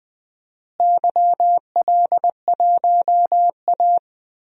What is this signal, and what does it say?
Morse code, heard as a keyed tone, about 20 wpm, 700 Hz. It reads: YL1A
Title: DXCC-100 Call Signs - 4 Characters - Repeated 2x - 20wpm